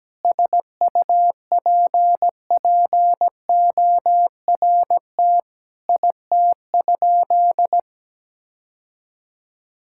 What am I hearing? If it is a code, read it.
Morse code, heard as a keyed tone, about 17 wpm, 700 Hz. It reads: SUPPORT IT?
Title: Sentences from Top 1000 Words - Review Part 5 - 17wpm